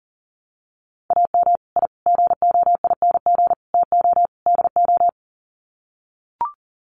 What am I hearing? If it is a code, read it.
Morse code, heard as a keyed tone, about 40 wpm, 700 Hz. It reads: AM I GOING TO DO
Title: Sets of 5 Words Encore 40wpm